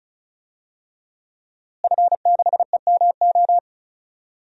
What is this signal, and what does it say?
Morse code, heard as a keyed tone, about 36 wpm, 700 Hz. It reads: F6EMO
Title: DXCC-100 Call Signs - 5 Characters - Repeated 3x - 35wpm